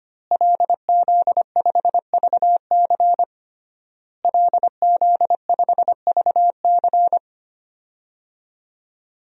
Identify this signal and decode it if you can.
Morse code, heard as a keyed tone, about 25 wpm, 700 Hz. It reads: LZ5VC LZ5VC
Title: DXCC-100 Call Signs - 5 Characters - Repeated 3x - 25wpm